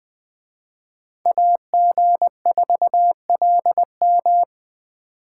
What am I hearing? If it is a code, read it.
Morse code, heard as a keyed tone, about 20 wpm, 700 Hz. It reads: AG4LM
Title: Speed Racing - USA Call Signs 17wpm